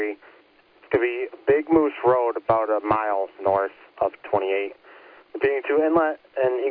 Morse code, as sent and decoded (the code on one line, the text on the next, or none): none